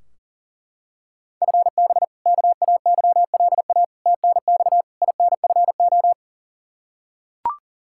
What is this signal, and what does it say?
Morse code, heard as a keyed tone, about 40 wpm, 700 Hz. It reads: FB KAYLA TNX INFO
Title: K1USN SST QSO Practice 40wpm